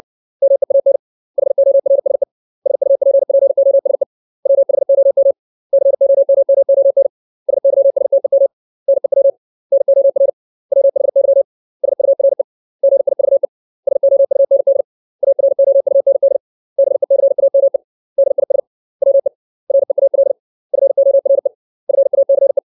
Morse code, heard as a keyed tone, about 60 wpm, 550 Hz. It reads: MEAN HORSE SUPPOSE WHOM COMMON SYSTEM NEW NOR WHY SURE GIVE SOUND AROUND BECAME DIS WE REAL FORE FACE
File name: 632 words / 632 Words 60WPM